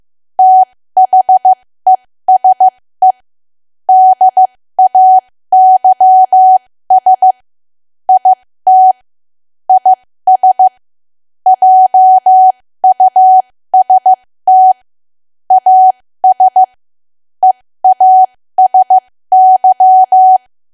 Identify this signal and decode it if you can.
Morse code, heard as a keyed tone, about 15 wpm, spaced as for 13 wpm, 750 Hz. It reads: THESE DAYS IT IS JUST AS EASY
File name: cwa / W1AW-Level2 / 13WPM